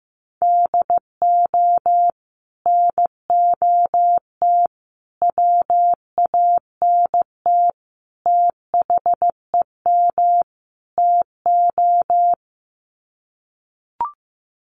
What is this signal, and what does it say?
Morse code, heard as a keyed tone, about 15 wpm, 700 Hz. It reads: DO NOT WANT THEM TO
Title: Sets of 5 Words Encore 15wpm